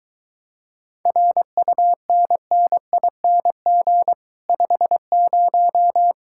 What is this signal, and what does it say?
Morse code, heard as a keyed tone, about 23 wpm, 700 Hz. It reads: RUNNING 50